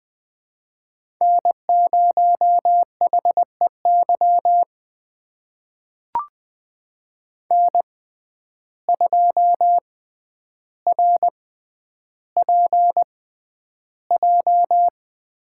Morse code, heard as a keyed tone, about 20 wpm, spaced as for 10 wpm, 700 Hz. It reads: N0HEY N2RPJ
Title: USA Call Signs - 5 Characters - 6x Character Spacing - 20wpm